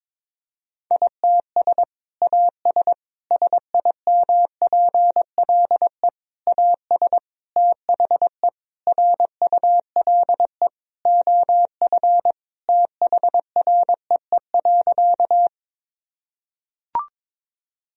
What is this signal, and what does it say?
Morse code, heard as a keyed tone, about 22 wpm, 700 Hz. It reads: ITS AS SIMPLE AS THE RULE OF THREE.
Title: Sentences from Top 1000 Words - Review Part 2 - 22wpm